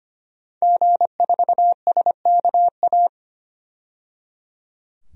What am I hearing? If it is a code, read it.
Morse code, heard as a keyed tone, about 25 wpm, 700 Hz. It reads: G4SKA
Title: DXCC-100 Call Signs - 5 Characters - 25wpm